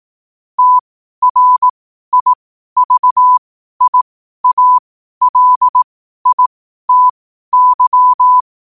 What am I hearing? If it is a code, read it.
Morse code, heard as a keyed tone, about 18 wpm, spaced as for 12 wpm, 1000 Hz. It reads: TRIVIALITY
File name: qotdfiles / 12wpm